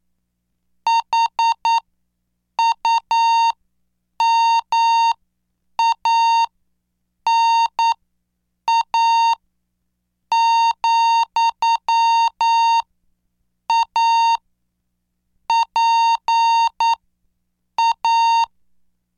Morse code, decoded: HUMANA, A PA